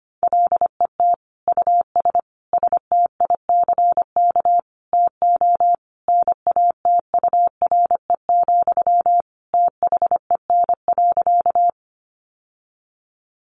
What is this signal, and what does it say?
Morse code, heard as a keyed tone, about 25 wpm, 700 Hz. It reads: LET US STICK TO NATURE, THEN.
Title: Sentences from Top 1200 Words 25wpm